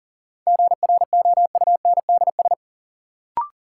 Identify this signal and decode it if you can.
Morse code, heard as a keyed tone, about 40 wpm, 700 Hz. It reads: GROUNDS
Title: Sentences from Top 1300 Words - New Word Drill Alt - 40wpm